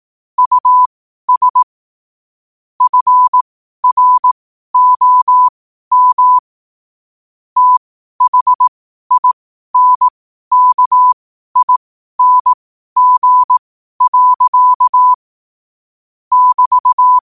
Morse code, heard as a keyed tone, about 18 wpm, spaced as for 12 wpm, 1000 Hz. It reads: US FROM THINKING. <BT>
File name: qotdfiles / 12wpm